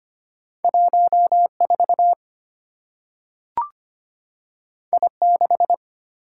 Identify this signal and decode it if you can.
Morse code, heard as a keyed tone, about 25 wpm, 700 Hz. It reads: 14 I6
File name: Two Letter-Number Combinations - 25wpm